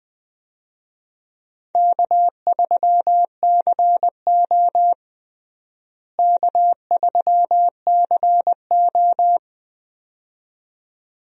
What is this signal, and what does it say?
Morse code, heard as a keyed tone, about 20 wpm, 700 Hz. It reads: K3CO K3CO
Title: USA Call Signs - 4 Characters - Repeated 2x - 20wpm